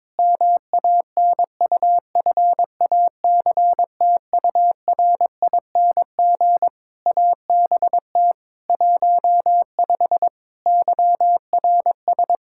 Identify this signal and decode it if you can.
Morse code, heard as a keyed tone, about 22 wpm, 700 Hz. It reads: MANUFACTURING ABT 15 YRS
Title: Practice QSO - Exchange 3 - More Info - 22wpm